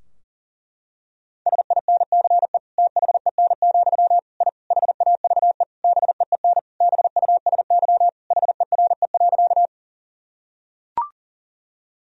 Morse code, heard as a keyed tone, about 40 wpm, 700 Hz. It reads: SINCE THEN, I HAVE BEEN BUSY HERE.
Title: Sentences from Top 1000 Words - Review Part 3 - 40wpm